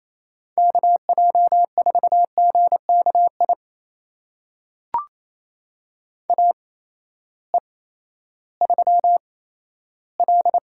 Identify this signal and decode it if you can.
Morse code, heard as a keyed tone, about 28 wpm, spaced as for 14 wpm, 700 Hz. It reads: KJ4GKI AE3L
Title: USA Call Signs - 8x Character Spacing - 28wpm